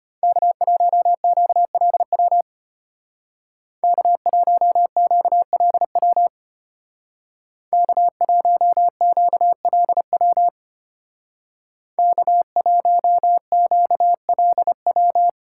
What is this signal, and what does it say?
Morse code, heard as a keyed tone, about 32 wpm, 700 Hz. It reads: K1QLW K1QLW K1QLW K1QLW